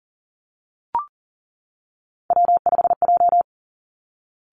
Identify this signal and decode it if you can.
Morse code, heard as a keyed tone, about 40 wpm, 700 Hz. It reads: W5J
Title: Three Challenging Letter-Number Combinations 40wpm